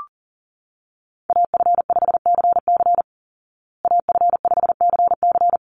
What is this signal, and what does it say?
Morse code, heard as a keyed tone, about 40 wpm, 700 Hz. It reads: AF5CC AF5CC
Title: Call Signs Repeated 3x - 40wpm